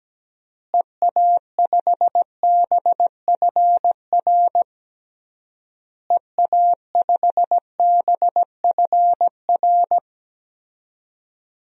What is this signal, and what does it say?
Morse code, heard as a keyed tone, about 17 wpm, 700 Hz. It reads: EA5BFR EA5BFR
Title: DXCC-100 Call Signs - 6 Characters - Repeated 3x - 17wpm